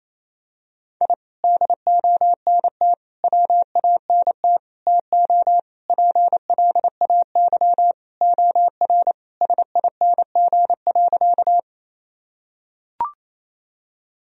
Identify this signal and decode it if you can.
Morse code, heard as a keyed tone, about 28 wpm, 700 Hz. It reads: I DONT WANT TO PLAY OR SING.
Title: Sentences from Top 1000 Words - Review Part 2 - 28wpm